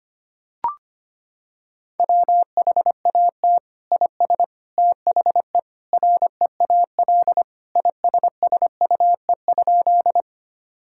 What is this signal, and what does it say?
Morse code, heard as a keyed tone, about 25 wpm, 700 Hz. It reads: WHAT IS THE REAL ISSUE?
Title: Sentences from Top 1000 Words - Review Part 6 - 25wpm